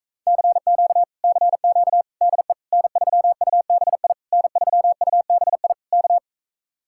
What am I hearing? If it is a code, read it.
Morse code, heard as a keyed tone, about 37 wpm, 700 Hz. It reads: CQ CQ DE N3UBI N3UBI K